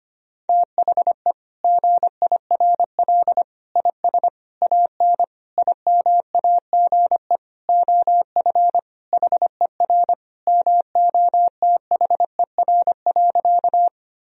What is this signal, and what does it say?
Morse code, heard as a keyed tone, about 25 wpm, 700 Hz. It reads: THE GIRL IS AN IMAGE OF HER MOTHER.
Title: Sentences from Top 1000 Words - Review Part 4 - 25wpm